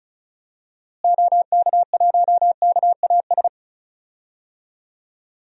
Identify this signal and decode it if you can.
Morse code, heard as a keyed tone, about 34 wpm, 700 Hz. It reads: OK1KAS
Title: DXCC-100 Call Signs - 6 Characters - Repeated 3x - 35wpm